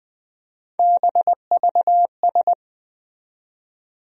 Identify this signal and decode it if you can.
Morse code, heard as a keyed tone, about 20 wpm, 700 Hz. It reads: BVS
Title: USA Call Sign Postfixes - 3 Letters 20wpm